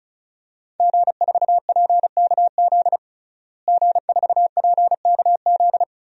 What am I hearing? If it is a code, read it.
Morse code, heard as a keyed tone, about 36 wpm, 700 Hz. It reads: G4PKZ G4PKZ
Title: DXCC-100 Call Signs - 5 Characters - Repeated 3x - 35wpm